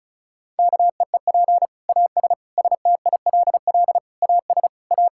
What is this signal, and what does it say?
Morse code, heard as a keyed tone, about 34 wpm, 700 Hz. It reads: KEEP AS STILL AS A
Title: Sentences from Top 1000 Words - Review Part 5 - 35wpm